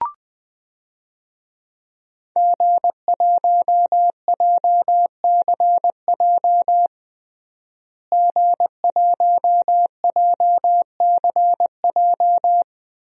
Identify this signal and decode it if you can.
Morse code, heard as a keyed tone, about 20 wpm, 700 Hz. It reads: G1JCJ G1JCJ
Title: DXCC-100 Call Signs - 5 Characters - Repeated 3x - 20wpm